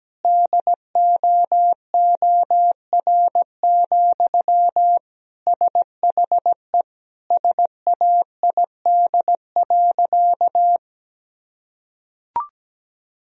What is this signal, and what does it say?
Morse code, heard as a keyed tone, about 17 wpm, 700 Hz. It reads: DOOR, SHE SAID.
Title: Sentences from Top 1000 Words - Review Part 2 - 17wpm